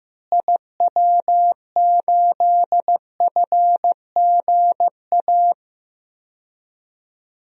Morse code, heard as a keyed tone, about 15 wpm, 700 Hz. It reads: IW8FGA
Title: DXCC-100 Call Signs - 6 Characters - Repeated 3x - 15wpm